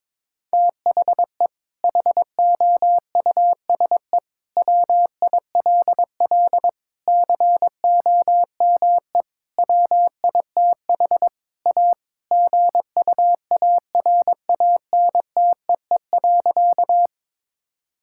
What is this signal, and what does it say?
Morse code, heard as a keyed tone, about 22 wpm, 700 Hz. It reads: THE HOUSE WILL COME WITH A GUARANTEE.
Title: Sentences from Top 1600 Words 22wpm